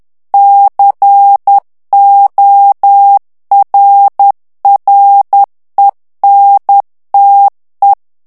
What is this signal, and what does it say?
Morse code, heard as a keyed tone, about 10 wpm, 800 Hz. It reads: CORRENTE
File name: morse mp3 / ITALIEN texte clair 1 vitesse 10 mots minute